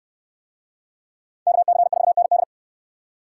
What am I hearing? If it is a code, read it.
Morse code, heard as a keyed tone, about 67 wpm, 700 Hz. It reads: KC4ND